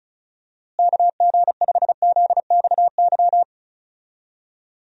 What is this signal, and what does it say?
Morse code, heard as a keyed tone, about 36 wpm, 700 Hz. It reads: KG5ZXY
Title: USA Call Signs - 6 Characters - Repeated 2x - 35wpm